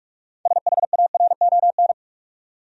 Morse code, heard as a keyed tone, about 46 wpm, 700 Hz. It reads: SHARON
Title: USA Common Names - 45wpm